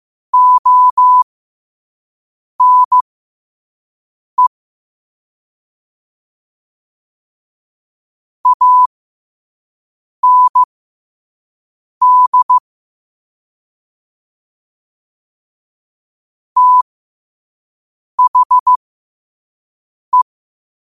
Morse code, heard as a keyed tone, about 15 wpm, spaced as for 5 wpm, 1000 Hz. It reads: ONE AND THE
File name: qotdfiles / 5wpm